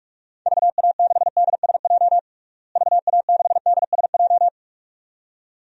Morse code, heard as a keyed tone, about 41 wpm, 700 Hz. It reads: VA6DSJ VA6DSJ
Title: DXCC-100 Call Signs - 6 Characters - Repeated 2x - 45wpm